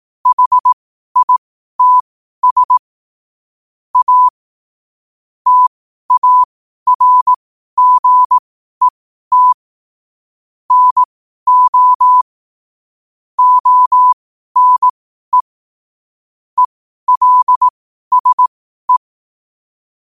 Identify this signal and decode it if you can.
Morse code, heard as a keyed tone, about 18 wpm, spaced as for 12 wpm, 1000 Hz. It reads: HITS A TARGET NO ONE ELSE
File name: qotdfiles / 12wpm